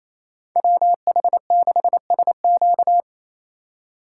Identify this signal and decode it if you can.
Morse code, heard as a keyed tone, about 36 wpm, 700 Hz. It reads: WH6SQ